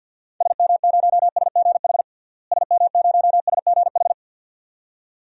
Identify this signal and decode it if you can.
Morse code, heard as a keyed tone, about 51 wpm, 700 Hz. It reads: SM0SGH SM0SGH
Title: DXCC-100 Call Signs - 6 Characters - Repeated 2x - 50wpm